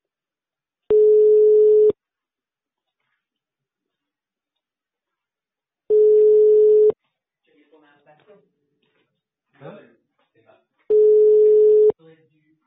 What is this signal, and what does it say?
Ringback tone (Europe, 425 Hz), with speech.